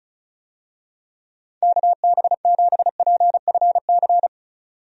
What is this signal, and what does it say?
Morse code, heard as a keyed tone, about 36 wpm, 700 Hz. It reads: KB7PFC
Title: USA Call Signs - 6 Characters - 35wpm